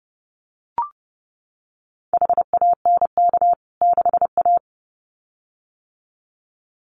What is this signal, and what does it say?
Morse code, heard as a keyed tone, about 30 wpm, 700 Hz. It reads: HANK 6A
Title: CWT Exchange 30wpm